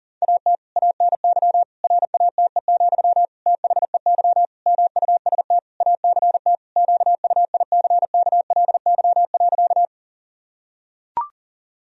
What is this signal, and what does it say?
Morse code, heard as a keyed tone, about 40 wpm, 700 Hz. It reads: AT ANY RATE, THEY MUST ACT QUICKLY.
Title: Sentences from Top 1000 Words - Review Part 4 - 40wpm